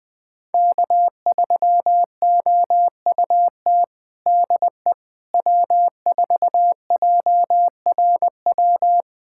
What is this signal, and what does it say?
Morse code, heard as a keyed tone, about 20 wpm, 700 Hz. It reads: K3OUT DE W4JRW
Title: USA Callsign DE Callsign - 5 Characters - 20wpm